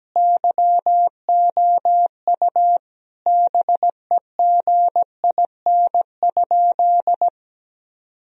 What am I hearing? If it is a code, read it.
Morse code, heard as a keyed tone, about 17 wpm, 700 Hz. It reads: YOU BEGIN?
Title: Sentences from Top 1000 Words - Review Part 1 - 17wpm